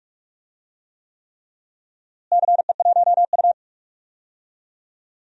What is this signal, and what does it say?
Morse code, heard as a keyed tone, about 46 wpm, 700 Hz. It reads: CE1U
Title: DXCC-100 Call Signs - 4 Characters - Repeated 3x - 45wpm